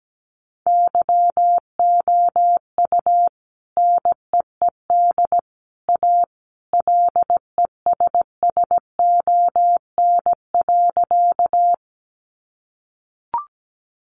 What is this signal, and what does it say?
Morse code, heard as a keyed tone, about 17 wpm, 700 Hz. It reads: YOU NEED A LESSON.
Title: Sentences from Top 1200 Words 17wpm